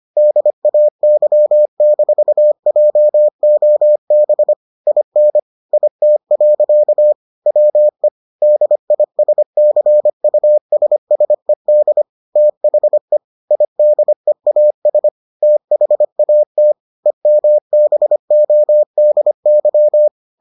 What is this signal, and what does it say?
Morse code, heard as a keyed tone, about 25 wpm, 600 Hz. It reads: DAY-JOB IN IT. WE DISCUSSED THE IDEAS THAT EMBODY